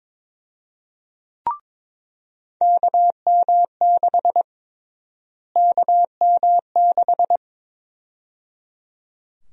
Morse code, heard as a keyed tone, about 21 wpm, 700 Hz. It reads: KM6 KM6
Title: USA Call Sign Prefixes - Repeated 2x - 22wpm